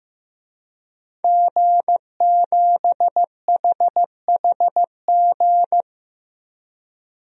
Morse code, heard as a keyed tone, about 15 wpm, 700 Hz. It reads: G7HHG
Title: DXCC-100 Call Signs - 5 Characters - 15wpm